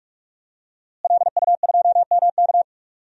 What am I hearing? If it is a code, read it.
Morse code, heard as a keyed tone, about 46 wpm, 700 Hz. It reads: LU2MK
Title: DXCC-100 Call Signs - 5 Characters - Repeated 3x - 45wpm